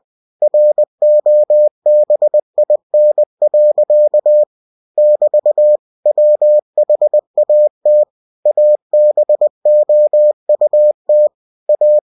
Morse code, heard as a keyed tone, about 20 wpm, 600 Hz. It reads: ROBIN. <BT> WHAT ABOUT A